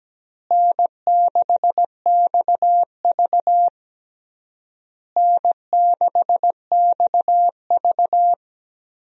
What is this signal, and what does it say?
Morse code, heard as a keyed tone, about 17 wpm, 700 Hz. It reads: N6XV N6XV
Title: USA Call Signs - 4 Characters - Repeated 3x - 17wpm